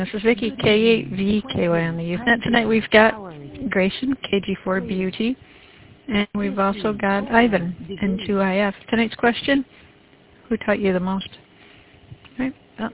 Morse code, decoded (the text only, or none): none